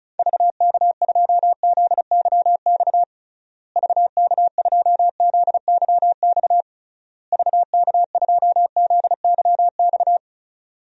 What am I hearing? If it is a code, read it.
Morse code, heard as a keyed tone, about 36 wpm, 700 Hz. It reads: VK2ZYX VK2ZYX VK2ZYX
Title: DXCC-100 Call Signs - 6 Characters - Repeated 3x - 35wpm